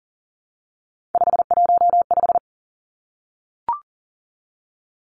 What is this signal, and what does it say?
Morse code, heard as a keyed tone, about 42 wpm, 700 Hz. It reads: 515
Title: Three Challenging Letter-Number Combinations 40wpm